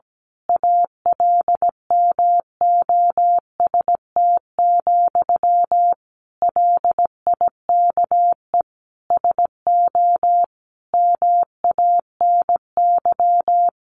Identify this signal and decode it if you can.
Morse code, heard as a keyed tone, about 17 wpm, 700 Hz. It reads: ALMOST, LIKE SO MANY
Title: A Sound Of Thunder 17wpm